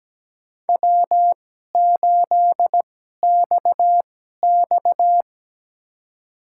Morse code, heard as a keyed tone, about 17 wpm, spaced as for 12 wpm, 700 Hz. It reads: W8XX
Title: USA Call Signs - 2x Character Spacing - 17wpm